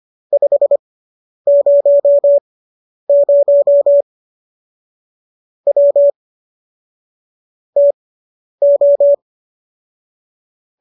Morse code, heard as a keyed tone, about 25 wpm, spaced as for 10 wpm, 570 Hz. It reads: 500 W TO